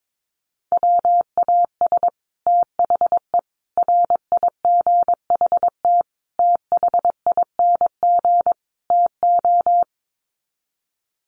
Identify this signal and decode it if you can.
Morse code, heard as a keyed tone, about 22 wpm, 700 Hz. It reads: WAS THE RIGHT THING TO
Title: Sets of 5 Words Encore2 22wpm